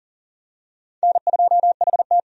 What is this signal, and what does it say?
Morse code, heard as a keyed tone, about 40 wpm, 700 Hz. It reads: N2HT
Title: USA Call Signs - 4 Characters - 40wpm